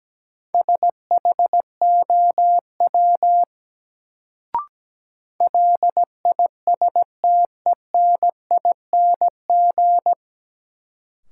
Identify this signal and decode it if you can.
Morse code, heard as a keyed tone, about 17 wpm, 700 Hz. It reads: SHOW LISTENING
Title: Sets of 5 Words - New Word Drill - 17wpm